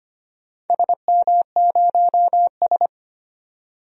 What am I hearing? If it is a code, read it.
Morse code, heard as a keyed tone, about 25 wpm, 700 Hz. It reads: SM0S
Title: DXCC-100 Call Signs - 4 Characters - 25wpm